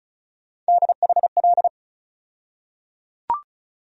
Morse code, heard as a keyed tone, about 36 wpm, 700 Hz. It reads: DHL
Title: USA Call Sign Postfixes - 3 Letters 35wpm